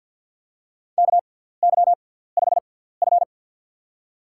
Morse code, heard as a keyed tone, about 52 wpm, spaced as for 17 wpm, 700 Hz. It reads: KY5F
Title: USA Call Signs - 4 Characters - 6x Character Spacing - 50wpm